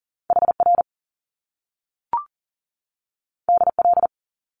Morse code, heard as a keyed tone, about 42 wpm, 700 Hz. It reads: HR DL
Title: Top Two Letter Combinations in Words 40wpm